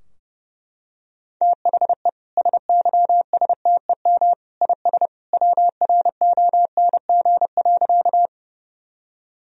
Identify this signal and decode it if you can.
Morse code, heard as a keyed tone, about 28 wpm, 700 Hz. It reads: THE SYSTEM IS WRONG.
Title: Sentences from Top 1000 Words - Review Part 3 - 30wpm